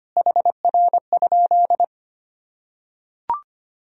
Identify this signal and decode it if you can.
Morse code, heard as a keyed tone, about 25 wpm, 700 Hz. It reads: HR?
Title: Sentences from Top 200 Words and Abbreviations 25wpm